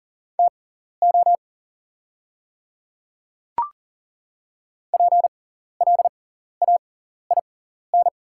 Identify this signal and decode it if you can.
Morse code, heard as a keyed tone, about 40 wpm, spaced as for 14 wpm, 690 Hz. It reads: TO PLAIN
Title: 6x Character Spacing - Top 500 Words - 40wpm